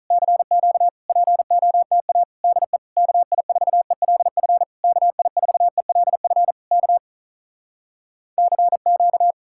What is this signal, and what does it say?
Morse code, heard as a keyed tone, about 46 wpm, 700 Hz. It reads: CQ POTA DE KI4ELF KI4ELF K CQ